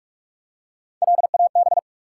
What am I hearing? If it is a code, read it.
Morse code, heard as a keyed tone, about 44 wpm, 700 Hz. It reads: LAB
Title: Three Letter Words - 45wpm